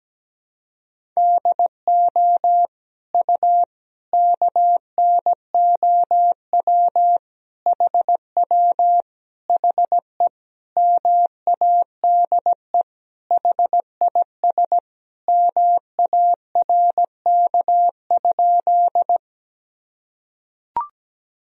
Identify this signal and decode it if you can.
Morse code, heard as a keyed tone, about 17 wpm, 700 Hz. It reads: DO U KNOW HW HE MADE HIS MARK?
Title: Sentences from Top 300 Words and Abbreviations 17wpm